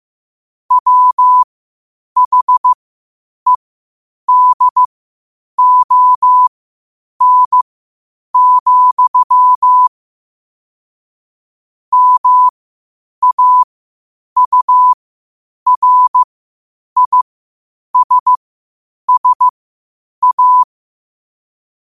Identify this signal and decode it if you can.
Morse code, heard as a keyed tone, about 15 wpm, spaced as for 8 wpm, 1000 Hz. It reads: WHEDON, MAURISSA
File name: qotdfiles / 8wpm